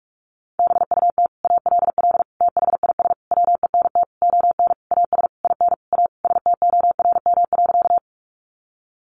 Morse code, heard as a keyed tone, about 44 wpm, 700 Hz. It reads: BUT ALL THIS WENT ON AS IN A STORM.
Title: Sentences from Top 1500 Words 45wpm